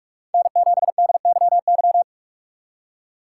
Morse code, heard as a keyed tone, about 46 wpm, 700 Hz. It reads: N7DYY
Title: USA Call Signs - 5 Characters - 6x Character Spacing - 45wpm